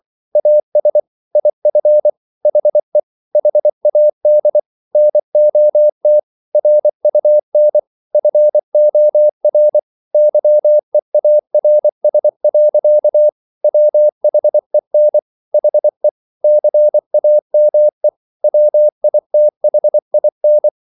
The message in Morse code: AS IF HE HAD NOT RUN FOR YEARS. WHEN HE CAME WITHIN

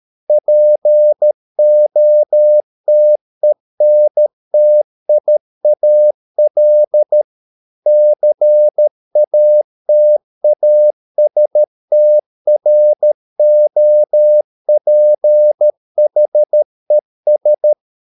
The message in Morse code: POTENTIAL CATASTROPHES

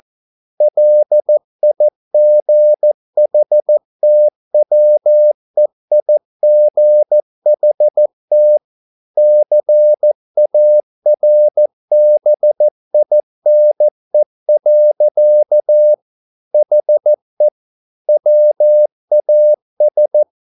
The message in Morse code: LIGHTWEIGHT CARBINE. HE WAS

